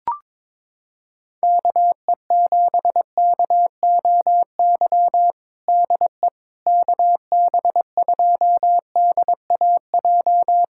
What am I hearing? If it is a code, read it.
Morse code, heard as a keyed tone, about 22 wpm, 700 Hz. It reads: KE7KOY DE KB2DAJ
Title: USA Callsign DE Callsign - 6 Characters - 22wpm